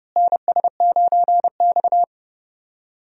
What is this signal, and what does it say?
Morse code, heard as a keyed tone, about 30 wpm, 700 Hz. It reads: NS9X
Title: USA Call Signs - 4 Characters - Repeated 2x - 30wpm